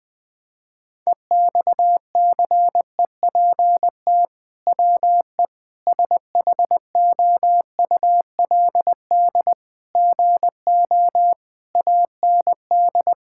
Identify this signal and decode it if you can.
Morse code, heard as a keyed tone, about 20 wpm, 700 Hz. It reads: EXCEPT WE SHOULD GO AND
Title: Sentences from Top 1600 Words 20wpm